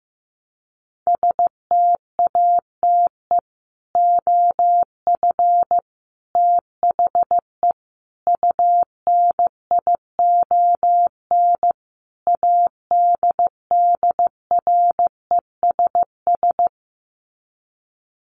Morse code, heard as a keyed tone, about 15 wpm, 700 Hz. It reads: STATE OF THE UNION ADDRESS
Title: Sets of 5 Words Encore 15wpm